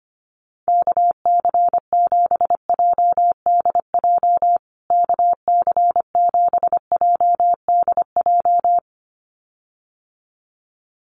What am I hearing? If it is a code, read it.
Morse code, heard as a keyed tone, about 25 wpm, 700 Hz. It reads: KC7JDJ KC7JDJ
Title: USA Call Signs Repeated 2x 25wpm